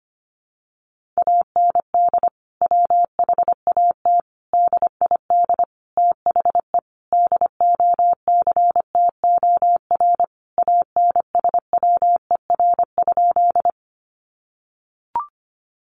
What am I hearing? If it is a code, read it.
Morse code, heard as a keyed tone, about 25 wpm, 700 Hz. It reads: AND WHAT DID THE DOCTOR ANSWER?
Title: Sentences from Top 1500 Words 25wpm